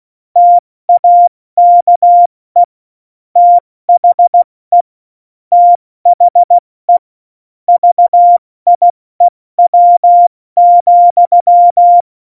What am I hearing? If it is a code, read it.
Morse code, heard as a keyed tone, about 16 wpm, 700 Hz. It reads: TAKE THE THE VIEW,
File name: Morsekurs / Witze / Witz 25 Juli 17, Tempo 70 F80